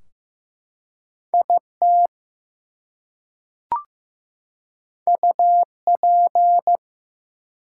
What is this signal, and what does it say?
Morse code, heard as a keyed tone, about 15 wpm, 700 Hz. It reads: IT UP
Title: Two Letter Words - 15wpm